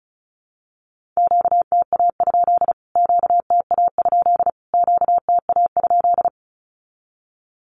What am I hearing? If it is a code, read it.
Morse code, heard as a keyed tone, about 34 wpm, 700 Hz. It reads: QTA? QTA? QTA?
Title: Q-Codes Repeated 3x 35wpm